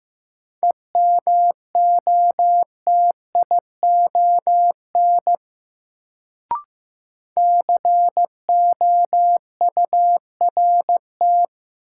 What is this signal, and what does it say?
Morse code, heard as a keyed tone, about 15 wpm, 700 Hz. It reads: EMOTION COURT
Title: Sentences from Top 1600 Words - New Word Drill Alt - 15wpm